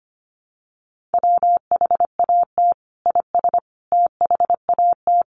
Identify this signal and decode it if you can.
Morse code, heard as a keyed tone, about 25 wpm, 700 Hz. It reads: WHAT IS THAT
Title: Sentences from Top 1400 Words 25wpm